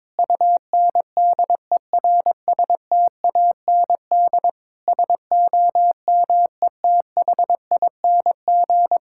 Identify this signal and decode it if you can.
Morse code, heard as a keyed tone, about 22 wpm, 700 Hz. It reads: UNDERSTAND SOMETHING
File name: Sentences with 9-Word Phrases - 22wpm